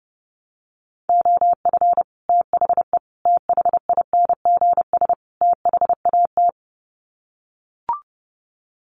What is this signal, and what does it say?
Morse code, heard as a keyed tone, about 30 wpm, 700 Hz. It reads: OF THE THINGS THAT
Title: Sets of 4 Words 30wpm